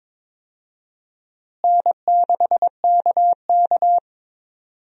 Morse code, heard as a keyed tone, about 22 wpm, 700 Hz. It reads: N6KK
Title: USA Call Signs - 4 Characters - Repeated 2x - 22wpm